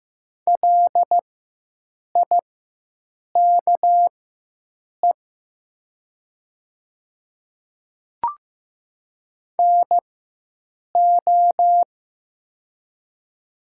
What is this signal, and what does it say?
Morse code, heard as a keyed tone, about 15 wpm, spaced as for 7 wpm, 700 Hz. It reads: LIKE NO
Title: 4x Character Spacing - Top 200 Words - 15wpm